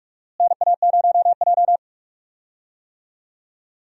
Morse code, heard as a keyed tone, about 42 wpm, 700 Hz. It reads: NA0J
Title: USA Call Signs - 4 Characters - Repeated 2x - 45wpm